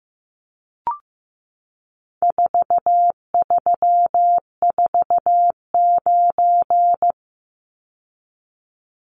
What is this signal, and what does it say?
Morse code, heard as a keyed tone, about 15 wpm, 700 Hz. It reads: 4349
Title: Numbers - 4-Digit 15wpm